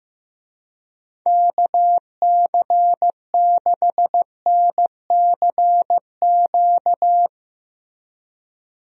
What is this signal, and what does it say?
Morse code, heard as a keyed tone, about 15 wpm, 700 Hz. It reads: KC6NCQ
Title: USA Call Signs - 6 Characters - Repeated 2x - 15wpm